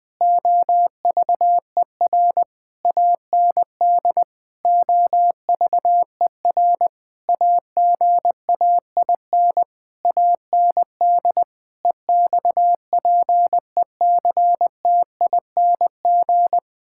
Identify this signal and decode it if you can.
Morse code, heard as a keyed tone, about 20 wpm, 700 Hz. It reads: OVER AND OVER AGAIN AND EXPECTING
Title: Sentences with 8-Word Phrases - 20wpm